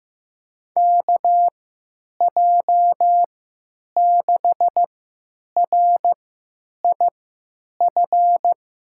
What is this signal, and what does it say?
Morse code, heard as a keyed tone, about 15 wpm, spaced as for 9 wpm, 700 Hz. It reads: KJ6RIF
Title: USA Call Signs - 6 Characters - 3x Character Spacing - 15wpm